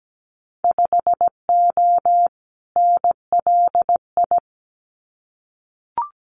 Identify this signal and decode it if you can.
Morse code, heard as a keyed tone, about 17 wpm, 700 Hz. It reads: 5O NLI
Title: Winter Field Day Exchanges 17wpm